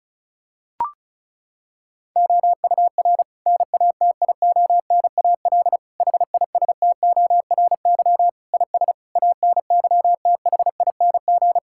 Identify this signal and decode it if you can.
Morse code, heard as a keyed tone, about 36 wpm, 700 Hz. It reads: OUR NATIONAL HISTORY IS ANYTHING
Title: Sentences from Top 1000 Words - Review Part 6 - 35wpm